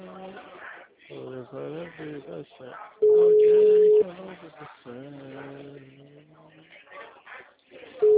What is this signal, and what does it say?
Ringback tone (Europe, 425 Hz), with speech.